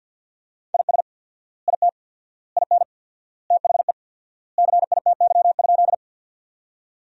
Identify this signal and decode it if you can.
Morse code, heard as a keyed tone, about 47 wpm, 700 Hz. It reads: IS IT IN THE CITY?
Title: 4x Word-Spacing - Sentences from Top 300 Words - 50wpm